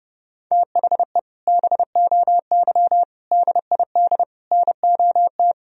Morse code, heard as a keyed tone, about 30 wpm, 700 Hz. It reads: THE BOY DID NOT
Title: Sentences from Top 1000 Words - Review Part 2 - 30wpm